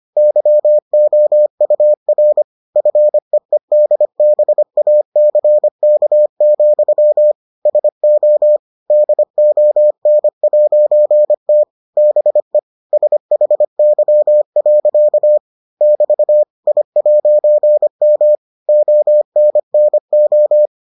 Morse code, heard as a keyed tone, about 25 wpm, 600 Hz. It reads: YOUR FEEDBACK, SO DON'T BE SHY. <BT> I'M ONNO